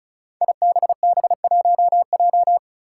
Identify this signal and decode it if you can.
Morse code, heard as a keyed tone, about 34 wpm, 700 Hz. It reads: IBB1J
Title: Five Challenging Letter-Number Combinations 35wpm